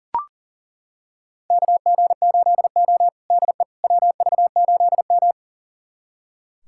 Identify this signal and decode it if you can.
Morse code, heard as a keyed tone, about 37 wpm, 700 Hz. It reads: KG8O DE WV8M
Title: USA Callsign DE Callsign - 4 Characters - 40wpm